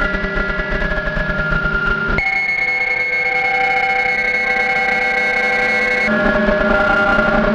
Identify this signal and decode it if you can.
PSK31, about 31.25 baud.
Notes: The radio was tuned to the 19 meters band, I didn’t interfere at all... so technically, it‘s a field recording.